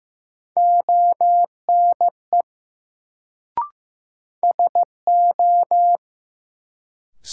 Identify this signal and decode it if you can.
Morse code, heard as a keyed tone, about 15 wpm, 700 Hz. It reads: ONE SO
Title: Sentences from Top 100 Words - New Word Drill - 15wpm